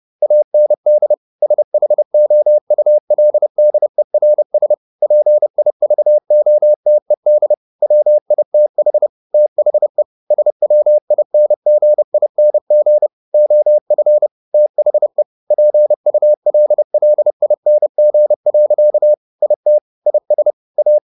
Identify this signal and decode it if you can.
Morse code, heard as a keyed tone, about 30 wpm, 600 Hz. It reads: AND SHOULDERS PIVOTED WITH THE SWINGING OF THE PULLING. IT IS A